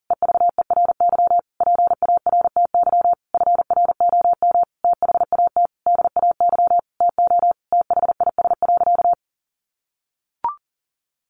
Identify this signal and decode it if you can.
Morse code, heard as a keyed tone, about 40 wpm, 700 Hz. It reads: EVERY PARTY FROM THAT DAY TO THIS.
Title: Sentences from Top 1400 Words 40wpm